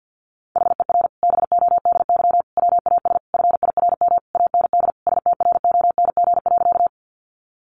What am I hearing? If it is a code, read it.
Morse code, heard as a keyed tone, about 50 wpm, 700 Hz. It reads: HER BODY WAS FIRM AND STRONG.
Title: Sentences from Top 1400 Words 50wpm